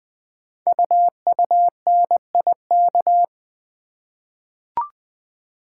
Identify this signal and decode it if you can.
Morse code, heard as a keyed tone, about 20 wpm, 700 Hz. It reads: UUNIK
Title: Code Groups - 5 Letters - 20wpm